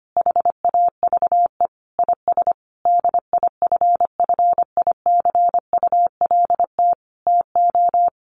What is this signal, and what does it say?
Morse code, heard as a keyed tone, about 25 wpm, 700 Hz. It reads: HAVE IS DIFFICULT TO
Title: Sentences from Top 1500 Words 25wpm